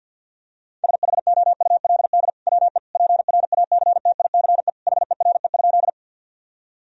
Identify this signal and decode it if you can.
Morse code, heard as a keyed tone, about 50 wpm, 700 Hz. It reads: SHOULD WE PRACTICE HERE?
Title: Sentences from Top 1000 Words - Review Part 4 - 50wpm